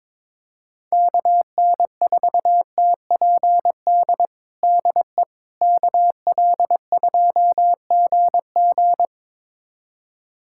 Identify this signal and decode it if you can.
Morse code, heard as a keyed tone, about 22 wpm, 700 Hz. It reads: KN4TPD DE KL2GG
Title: USA Callsign DE Callsign 22wpm